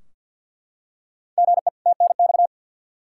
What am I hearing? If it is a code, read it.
Morse code, heard as a keyed tone, about 52 wpm, 700 Hz. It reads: GE TNX
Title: Practice QSO - Exchange 1 - Beginning - 50wpm